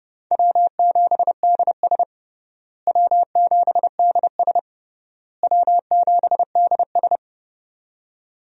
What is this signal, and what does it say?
Morse code, heard as a keyed tone, about 30 wpm, 700 Hz. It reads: W7DS W7DS W7DS
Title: USA Call Signs - 4 Characters - Repeated 3x - 30wpm